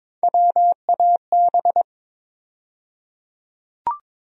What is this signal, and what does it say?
Morse code, heard as a keyed tone, about 22 wpm, 700 Hz. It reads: WAB
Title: USA Call Sign Postfixes - 3 Letters 22wpm